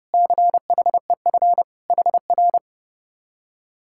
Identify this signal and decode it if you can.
Morse code, heard as a keyed tone, about 37 wpm, 700 Hz. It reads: CHEF HR
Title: Speed-Racing Common QSO Phrases Encore - 30wpm